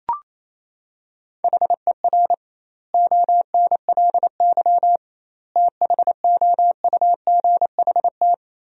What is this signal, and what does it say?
Morse code, heard as a keyed tone, about 28 wpm, 700 Hz. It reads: HER ONLY THOUGHT
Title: 2x Word-Spacing - Sentences from Top 400 Words - 28wpm